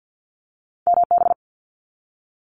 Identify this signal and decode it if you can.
Morse code, heard as a keyed tone, about 49 wpm, 700 Hz. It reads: MB
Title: Canadian Provinces and Territories - Repeated 3x - 50wpm